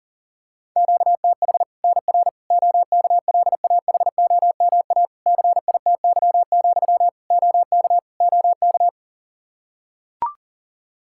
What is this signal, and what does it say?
Morse code, heard as a keyed tone, about 40 wpm, 700 Hz. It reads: QTH NR OKLAHOMA CITY, OK OK
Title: Practice QSO - Exchange 1 - Beginning - 40wpm